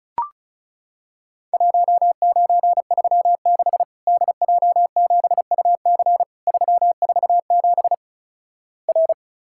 Morse code, heard as a keyed tone, about 36 wpm, 700 Hz. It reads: 1936 DJ7UC 347 R
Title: Worked All Europe DX Contest - QTC Exchange - 35wpm